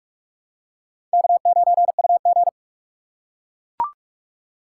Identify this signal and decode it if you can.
Morse code, heard as a keyed tone, about 42 wpm, 700 Hz. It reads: K9UG
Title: USA Call Signs - 4 Characters - 4x Character Spacing - 45wpm